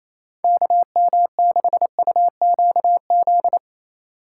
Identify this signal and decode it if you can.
Morse code, heard as a keyed tone, about 34 wpm, 700 Hz. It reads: KM6UQZ
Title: Speed Racing - USA 2x3 Call Signs 28wpm